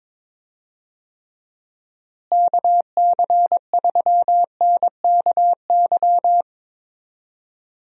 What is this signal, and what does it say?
Morse code, heard as a keyed tone, about 22 wpm, 700 Hz. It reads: KC3NKY
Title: USA Call Signs - 6 Characters - Repeated 3x - 22wpm